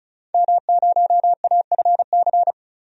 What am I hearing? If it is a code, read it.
Morse code, heard as a keyed tone, about 34 wpm, 700 Hz. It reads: M0AFC
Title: DXCC-100 Call Signs - 5 Characters - 35wpm